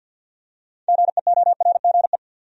Morse code, heard as a keyed tone, about 52 wpm, 700 Hz. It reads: GEORGE
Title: USA Common Names - 50wpm